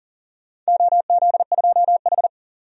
Morse code, heard as a keyed tone, about 40 wpm, 700 Hz. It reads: OZ2H
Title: DXCC-100 Call Signs - 4 Characters - 40wpm